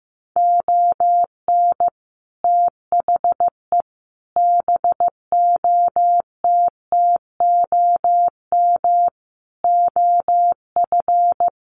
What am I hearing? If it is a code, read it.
Morse code, heard as a keyed tone, about 15 wpm, 700 Hz. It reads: ON THE BOTTOM OF
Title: Sets of 5 Words Encore2 15wpm